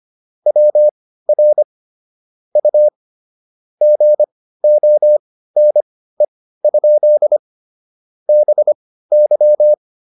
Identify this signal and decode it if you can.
Morse code, heard as a keyed tone, about 25 wpm, spaced as for 15 wpm, 600 Hz. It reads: WR U GONE? BY